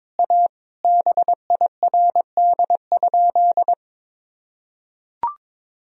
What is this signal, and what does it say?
Morse code, heard as a keyed tone, about 22 wpm, 700 Hz. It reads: A BIRD?
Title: Sentences from Top 1000 Words - Review Part 2 - 22wpm